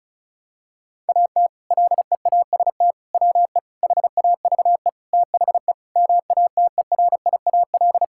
Morse code, heard as a keyed tone, about 34 wpm, 700 Hz. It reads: AT LEAST WE HAVE THE MATERIAL
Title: Sentences from Top 1000 Words - Review Part 5 - 35wpm